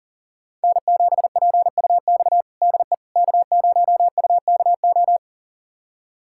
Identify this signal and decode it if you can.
Morse code, heard as a keyed tone, about 37 wpm, 700 Hz. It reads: N7PUX DE K0UKO
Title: USA Callsign DE Callsign - 5 Characters - 40wpm